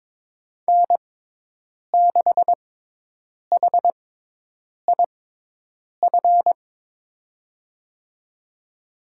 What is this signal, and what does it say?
Morse code, heard as a keyed tone, about 22 wpm, spaced as for 8 wpm, 700 Hz. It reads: N6HIF
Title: USA Call Signs - 5 Characters - 6x Character Spacing - 22wpm